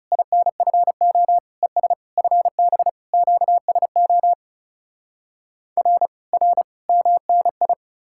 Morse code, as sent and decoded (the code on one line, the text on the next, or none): .. -. ..-. --- / . ... / ..-. -... / --.- ... --- / .-. / .-. / -- -. ..
INFO ES FB QSO R R MNI